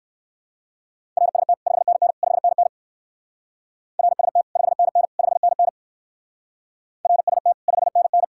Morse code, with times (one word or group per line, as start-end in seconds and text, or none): 1.2-1.5 RST
1.7-2.1 5NN
2.2-2.7 5NN
4.0-4.4 RST
4.6-5.0 5NN
5.2-5.7 5NN
7.0-7.5 RST
7.7-8.2 5NN